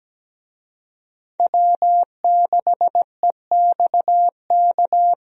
Morse code, heard as a keyed tone, about 17 wpm, 700 Hz. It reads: W6EXK
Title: USA Call Signs - 5 Characters - Repeated 2x - 17wpm